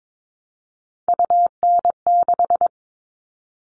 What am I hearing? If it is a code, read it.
Morse code, heard as a keyed tone, about 22 wpm, 700 Hz. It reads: UN6
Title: All Cut Numbers - 3-Digit 22wpm